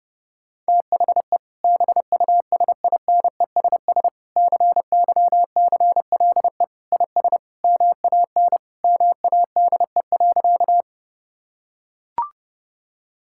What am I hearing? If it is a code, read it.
Morse code, heard as a keyed tone, about 30 wpm, 700 Hz. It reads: THE BUSINESS CYCLE IS MAN MADE.
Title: Sentences from Top 1000 Words - Review Part 6 - 30wpm